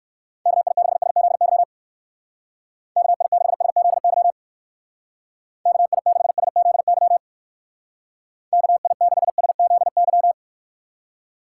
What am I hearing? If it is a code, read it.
Morse code, heard as a keyed tone, about 56 wpm, 700 Hz. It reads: KI6SZY KI6SZY KI6SZY KI6SZY
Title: Speed Racing - USA 2x3 Call Signs 45wpm